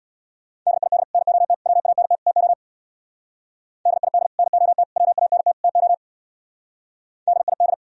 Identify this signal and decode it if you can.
Morse code, heard as a keyed tone, about 64 wpm, 700 Hz. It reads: DID NOT WANT TO DID NOT WANT TO DID